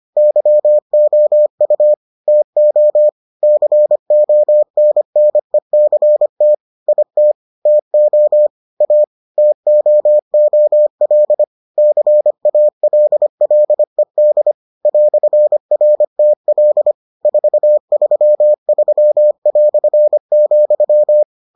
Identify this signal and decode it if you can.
Morse code, heard as a keyed tone, about 25 wpm, 600 Hz. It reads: YOU TO CONNECT IT TO A TOOL CALLED "RTL 433",